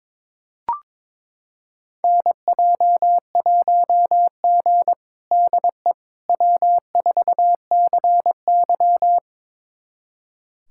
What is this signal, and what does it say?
Morse code, heard as a keyed tone, about 22 wpm, 700 Hz. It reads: NJ1G DE W4CY
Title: USA Callsign DE Callsign - 4 Characters - 22wpm